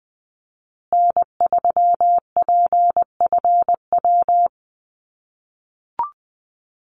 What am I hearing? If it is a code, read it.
Morse code, heard as a keyed tone, about 20 wpm, 700 Hz. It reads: N3PFW
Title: USA Call Signs Repeated 3x 20wpm